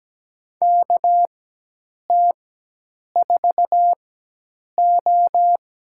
Morse code, heard as a keyed tone, about 17 wpm, spaced as for 8 wpm, 700 Hz. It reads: KT4O
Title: USA Call Signs - 4x Character Spacing - 17wpm